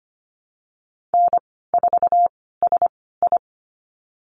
Morse code, heard as a keyed tone, about 25 wpm, spaced as for 16 wpm, 700 Hz. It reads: N4SI
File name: Lesson 15g - Number 5 - Call Signs 25wpm